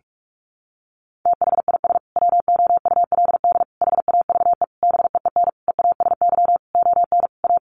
Morse code, heard as a keyed tone, about 44 wpm, 700 Hz. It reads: THIS WOULD HAVE BEEN EASY ON A
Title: Sentences from Top 800 Words 45wpm